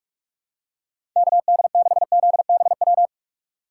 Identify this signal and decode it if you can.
Morse code, heard as a keyed tone, about 44 wpm, 700 Hz. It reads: KD6ZBW
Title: USA Call Signs - 6 Characters - 6x Character Spacing - 45wpm